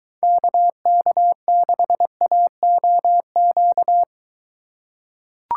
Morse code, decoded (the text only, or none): KK6AOQ